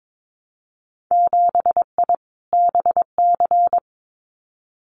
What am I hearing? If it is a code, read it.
Morse code, heard as a keyed tone, about 22 wpm, 700 Hz. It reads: 7I BC
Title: Winter Field Day Exchanges Encore 22wpm